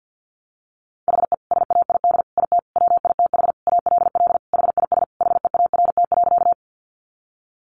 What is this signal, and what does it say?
Morse code, heard as a keyed tone, about 50 wpm, 700 Hz. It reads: HE SAID IT WITH ALL HIS HEART.
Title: Sentences from Top 1200 Words 50wpm